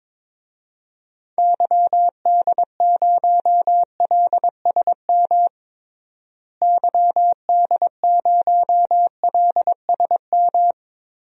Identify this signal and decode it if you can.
Morse code, heard as a keyed tone, about 22 wpm, 700 Hz. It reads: YD0LSM YD0LSM
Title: DXCC-100 Call Signs - 6 Characters - Repeated 3x - 22wpm